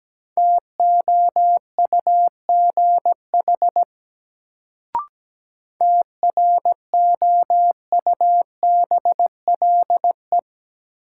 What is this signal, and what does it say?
Morse code, heard as a keyed tone, about 17 wpm, 700 Hz. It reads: TOUGH TROUBLE
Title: Sentences from Top 1200 Words - New Word Drill - 17wpm